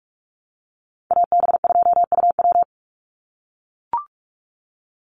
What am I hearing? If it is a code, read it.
Morse code, heard as a keyed tone, about 44 wpm, 700 Hz. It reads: AB2UW
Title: USA Call Signs 45wpm